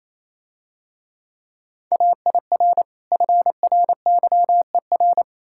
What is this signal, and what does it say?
Morse code, heard as a keyed tone, about 27 wpm, 690 Hz. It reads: AIR FRYER
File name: Easy Words - Speed-Racing with Hints - 22wpm